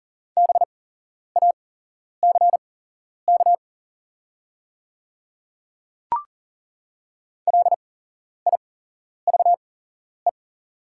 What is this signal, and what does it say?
Morse code, heard as a keyed tone, about 40 wpm, spaced as for 11 wpm, 700 Hz. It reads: BACK LIVE
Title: 8x Character Spacing - Top 200 Words - 40wpm